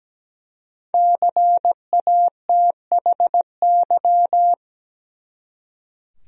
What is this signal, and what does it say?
Morse code, heard as a keyed tone, about 17 wpm, 700 Hz. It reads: CATHY
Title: USA Common Names - Max 5 Letters - 17wpm